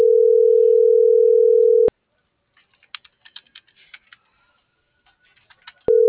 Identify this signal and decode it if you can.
Ringback tone (North America), 440 + 480 Hz.